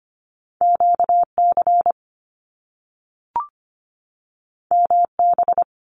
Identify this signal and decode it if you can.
Morse code, heard as a keyed tone, about 25 wpm, 700 Hz. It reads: QC MB
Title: Canadian Provinces and Territories 25wpm